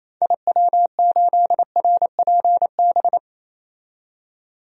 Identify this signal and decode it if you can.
Morse code, heard as a keyed tone, about 28 wpm, 700 Hz. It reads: IW8RPB
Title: DXCC-100 Call Signs - 6 Characters - 28wpm